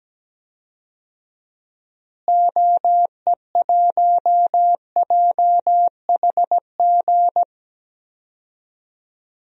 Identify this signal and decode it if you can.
Morse code, heard as a keyed tone, about 17 wpm, 700 Hz. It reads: OE1JHG
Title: DXCC-100 Call Signs - 6 Characters - Repeated 3x - 17wpm